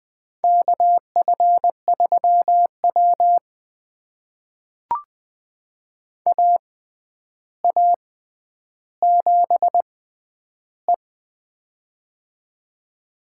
Morse code, heard as a keyed tone, about 20 wpm, spaced as for 10 wpm, 700 Hz. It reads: KF3W AA7E
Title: USA Call Signs - 4 Characters - 6x Character Spacing - 20wpm